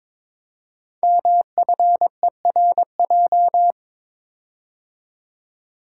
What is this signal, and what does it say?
Morse code, heard as a keyed tone, about 23 wpm, 700 Hz. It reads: MFERJ
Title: Code Groups - 5 Letters - 22wpm